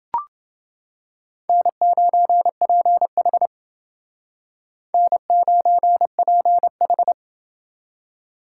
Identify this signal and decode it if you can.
Morse code, heard as a keyed tone, about 28 wpm, 700 Hz. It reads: N9PH N9PH